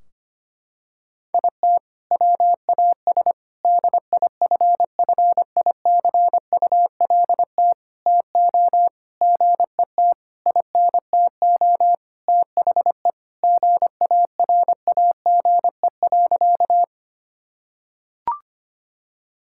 Morse code, heard as a keyed tone, about 25 wpm, 700 Hz. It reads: IT WAS DIFFICULT TO GET INTO THE GARAGE.
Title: Sentences from Top 1600 Words 25wpm